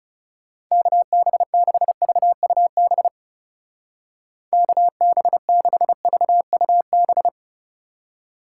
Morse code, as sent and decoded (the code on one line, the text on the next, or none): -.- -... -.... ...- ..- -... / -.- -... -.... ...- ..- -...
KB6VUB KB6VUB